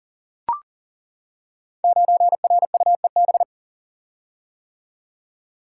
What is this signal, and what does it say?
Morse code, heard as a keyed tone, about 40 wpm, 690 Hz. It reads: 9RUEB
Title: Code Groups - 5 Letter:Numbers - 40wpm